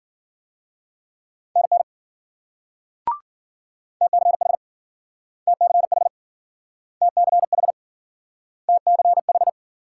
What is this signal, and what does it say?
Morse code, heard as a keyed tone, about 53 wpm, 700 Hz. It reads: NN TCH TCH TCH TCH